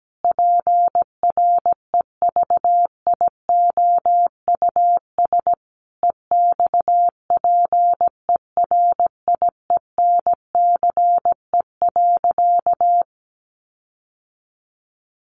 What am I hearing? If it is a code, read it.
Morse code, heard as a keyed tone, about 17 wpm, 700 Hz. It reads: PREVIOUS EXPERIENCE.
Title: Sentences from Top 1500 Words 17wpm